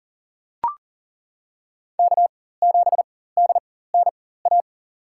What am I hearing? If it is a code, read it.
Morse code, heard as a keyed tone, about 40 wpm, spaced as for 19 wpm, 700 Hz. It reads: K7DNA
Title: USA Call Signs - 5 Characters - 4x Character Spacing - 40wpm